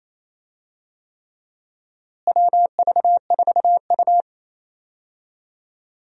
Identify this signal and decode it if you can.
Morse code, heard as a keyed tone, about 27 wpm, 700 Hz. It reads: WV4U
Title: USA Call Signs - 4 Characters - Repeated 3x - 28wpm